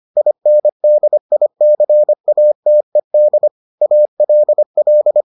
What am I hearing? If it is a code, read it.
Morse code, heard as a keyed tone, about 25 wpm, 600 Hz. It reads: INDICATED ALL